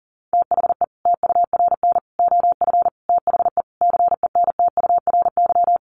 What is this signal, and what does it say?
Morse code, heard as a keyed tone, about 40 wpm, 700 Hz. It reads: THE TURN OF THE CENTURY
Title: Sets of 5 Words 40wpm